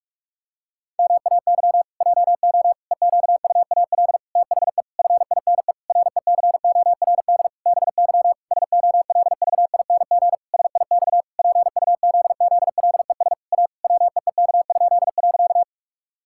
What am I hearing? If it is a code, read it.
Morse code, heard as a keyed tone, about 41 wpm, 700 Hz. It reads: MAY JO EQUAL THE FINE RECORD BY SOLVING SIX PUZZLES A WEEK?.
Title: Pangram Challenge v2 - 45wpm